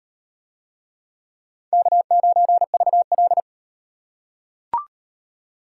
Morse code, heard as a keyed tone, about 38 wpm, 700 Hz. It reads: K9VL